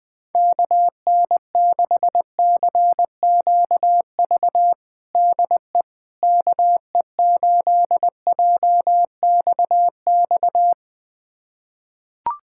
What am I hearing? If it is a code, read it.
Morse code, heard as a keyed tone, about 21 wpm, 700 Hz. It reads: KN6CQV DE KE8JXX
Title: USA Callsign DE Callsign - 6 Characters - 20wpm